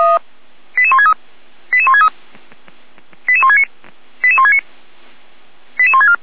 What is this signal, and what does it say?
Five-tone selective call (ZVEI), heard as quick runs of five tones, tones of about 70 ms. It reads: 88163 88163 88169 88169 88164